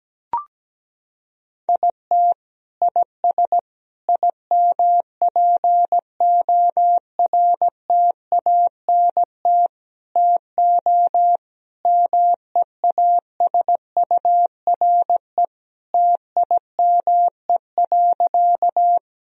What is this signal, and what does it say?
Morse code, heard as a keyed tone, about 17 wpm, 700 Hz. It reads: IT IS IMPORTANT TO MEASURE TIME.
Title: Sentences from Top 1000 Words - Review Part 2 - 17wpm